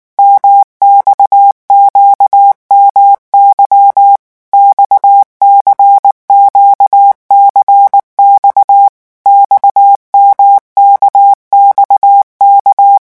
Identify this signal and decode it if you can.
Morse code, heard as a keyed tone, about 19 wpm, 800 Hz. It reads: MXQMY XCQCX XMKXK
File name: revision1-vitesse 18 mots